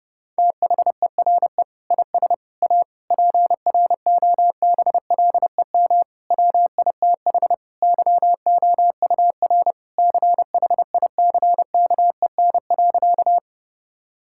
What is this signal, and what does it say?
Morse code, heard as a keyed tone, about 32 wpm, 700 Hz. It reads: THERE IS A PROBLEM WITH YOUR CHICKEN.
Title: Sentences from Top 1000 Words - Review Part 5 - 30wpm